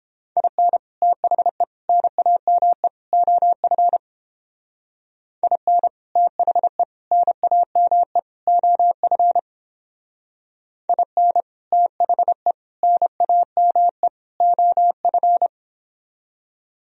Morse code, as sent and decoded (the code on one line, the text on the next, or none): .. -. / - .... / -. .- -- . / --- ..-. / .. -. / - .... / -. .- -- . / --- ..-. / .. -. / - .... / -. .- -- . / --- ..-.
IN THE NAME OF IN THE NAME OF IN THE NAME OF